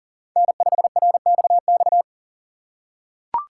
Morse code, heard as a keyed tone, about 42 wpm, 700 Hz. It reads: N5RXX
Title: USA Call Signs - 5 Characters - 6x Character Spacing - 40wpm